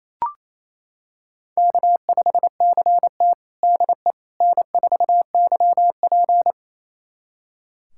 Morse code, heard as a keyed tone, about 28 wpm, 700 Hz. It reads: K5CT DE N4YP
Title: USA Callsign DE Callsign - 4 Characters - 28wpm